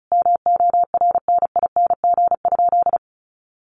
Morse code, heard as a keyed tone, about 34 wpm, 700 Hz. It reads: MORNING?
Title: Sentences from Top 1300 Words 35wpm